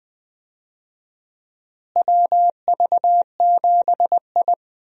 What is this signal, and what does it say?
Morse code, heard as a keyed tone, about 20 wpm, 700 Hz. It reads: WV7I
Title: USA Call Signs - 4 Characters - Repeated 3x - 20wpm